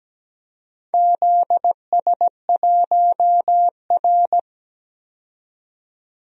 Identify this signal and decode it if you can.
Morse code, heard as a keyed tone, about 17 wpm, 700 Hz. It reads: ZS1R
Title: DXCC-100 Call Signs - 4 Characters - 17wpm